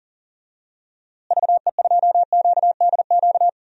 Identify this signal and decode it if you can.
Morse code, heard as a keyed tone, about 40 wpm, 700 Hz. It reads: VE2QDQ
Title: DXCC-100 Call Signs - 6 Characters - Repeated 2x - 40wpm